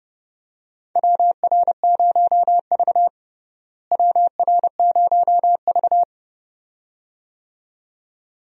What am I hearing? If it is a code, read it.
Morse code, heard as a keyed tone, about 31 wpm, 700 Hz. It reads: WR0V WR0V
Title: USA Call Signs - 4 Characters - Repeated 2x - 30wpm